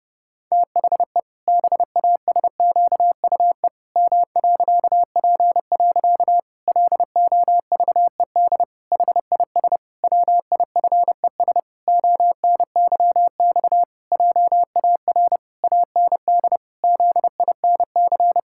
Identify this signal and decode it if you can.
Morse code, heard as a keyed tone, about 30 wpm, 700 Hz. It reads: THE BASQUE M.P. LOVED HIS WIFES ONYX JAR AND ZINC
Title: Pangram Challenge v2 - 30wpm